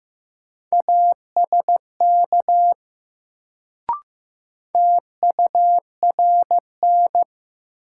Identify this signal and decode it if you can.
Morse code, heard as a keyed tone, about 15 wpm, 700 Hz. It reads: ASK TURN
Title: Sets of 4 Words - New Word Drill - 15wpm